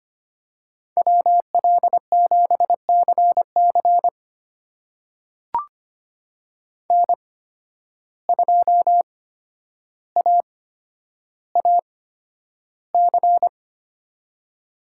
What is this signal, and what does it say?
Morse code, heard as a keyed tone, about 25 wpm, spaced as for 11 wpm, 700 Hz. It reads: WL7CC N2AAC
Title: USA Call Signs - 5 Characters - 8x Character Spacing - 25wpm